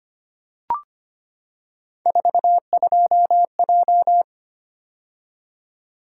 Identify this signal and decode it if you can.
Morse code, heard as a keyed tone, about 25 wpm, 700 Hz. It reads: 42J
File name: Three Letter-Number Combinations - 25wpm